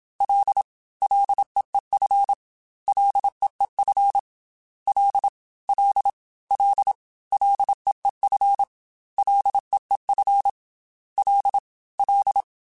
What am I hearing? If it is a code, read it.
Morse code, heard as a keyed tone, about 27 wpm, 800 Hz. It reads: L LEEF LEEF L L L LEEF LEEF L L
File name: F and L / F and L mp3 / F and L Prefixes 3x2 Twice @25